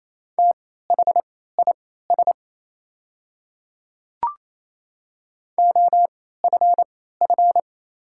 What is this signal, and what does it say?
Morse code, heard as a keyed tone, about 28 wpm, spaced as for 16 wpm, 700 Hz. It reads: THIS OFF
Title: 3x Character Spacing - Top 200 Words - 28wpm